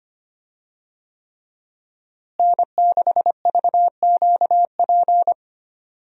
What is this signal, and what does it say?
Morse code, heard as a keyed tone, about 25 wpm, 700 Hz. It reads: N6VQP
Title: USA Call Signs - 5 Characters - Repeated 3x - 25wpm